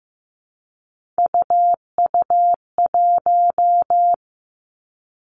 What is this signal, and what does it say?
Morse code, heard as a keyed tone, about 15 wpm, 700 Hz. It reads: UU1
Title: Three Challenging Letter-Number Combinations 15wpm